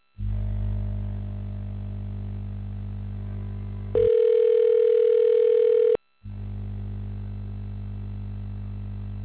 Ringback tone (North America), 440 + 480 Hz.